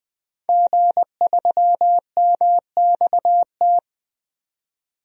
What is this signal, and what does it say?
Morse code, heard as a keyed tone, about 20 wpm, 700 Hz. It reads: G3MXT
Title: DXCC-100 Call Signs - 5 Characters - 20wpm